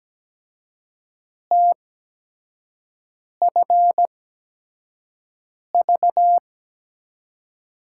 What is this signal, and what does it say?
Morse code, heard as a keyed tone, about 17 wpm, spaced as for 5 wpm, 700 Hz. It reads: TFV
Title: USA Call Signs - 6 Characters - 8x Character Spacing - 17wpm